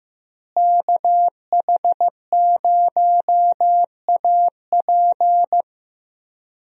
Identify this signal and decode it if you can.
Morse code, heard as a keyed tone, about 16 wpm, 700 Hz. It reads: KH0AP
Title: USA Call Signs - 5 Characters - Repeated 3x - 15wpm